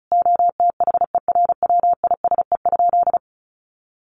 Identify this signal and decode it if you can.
Morse code, heard as a keyed tone, about 34 wpm, 700 Hz. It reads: OTHERWISE?
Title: Sentences from Top 1200 Words 35wpm